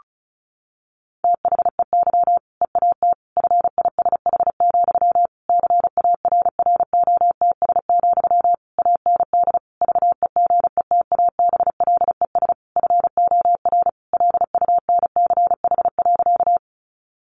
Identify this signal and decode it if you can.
Morse code, heard as a keyed tone, about 34 wpm, 700 Hz. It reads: THEY EAT FISH, CARROTS, AND VEGETABLES FOR LUNCH.
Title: Easy Phrases 35wpm